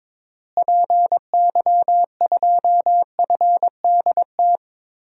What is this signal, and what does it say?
Morse code, heard as a keyed tone, about 22 wpm, 700 Hz. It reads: PY2FDT
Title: DXCC-100 Call Signs - 6 Characters - Repeated 2x - 22wpm